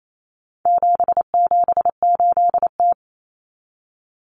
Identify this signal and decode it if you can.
Morse code, heard as a keyed tone, about 28 wpm, 700 Hz. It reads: 778T
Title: Common Cut Numbers - 4-Digit 28wpm